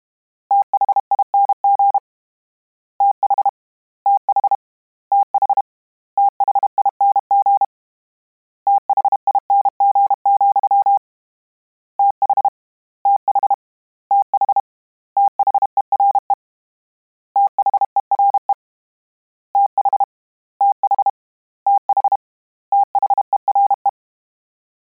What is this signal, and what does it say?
Morse code, heard as a keyed tone, about 32 wpm, 800 Hz. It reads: THING TH TH TH THING THING, TH TH TH THERE THERE TH TH TH THERE
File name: Combinations / Combinations-mp3s / Combinations@30